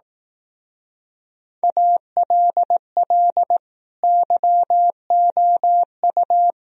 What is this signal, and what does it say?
Morse code, heard as a keyed tone, about 18 wpm, 700 Hz. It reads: ALL YOU
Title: Speed-Racing - Sets of 5 Words 15wpm